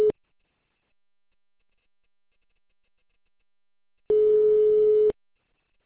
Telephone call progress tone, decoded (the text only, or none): ringback tone